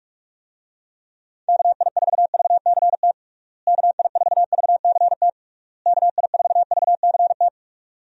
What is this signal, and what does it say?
Morse code, heard as a keyed tone, about 41 wpm, 700 Hz. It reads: KI4VCT KI4VCT KI4VCT
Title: USA Call Signs - 6 Characters - Repeated 3x - 45wpm